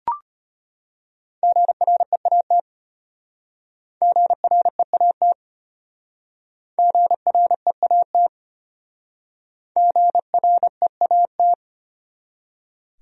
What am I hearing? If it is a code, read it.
Morse code, heard as a keyed tone, about 32 wpm, 700 Hz. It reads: GREAT GREAT GREAT GREAT